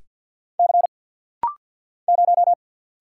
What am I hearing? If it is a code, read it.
Morse code, heard as a keyed tone, about 48 wpm, 700 Hz. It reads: C 0
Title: Single Letter-Number - Rapid-Fire 50wpm